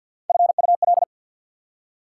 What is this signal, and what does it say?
Morse code, heard as a keyed tone, about 48 wpm, 700 Hz. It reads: FUL
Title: Prefixes and Suffixes 50wpm